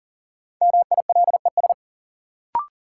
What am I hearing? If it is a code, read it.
Morse code, heard as a keyed tone, about 40 wpm, 700 Hz. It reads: MILES
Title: Sentences from Top 1000 Words - Review Part 3 - New Word Drill - 40wpm